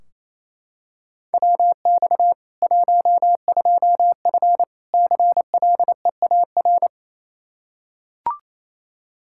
Morse code, heard as a keyed tone, about 28 wpm, 700 Hz. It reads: WX 12F CLEAR
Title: Practice QSO - Exchange 2 - Info - 28wpm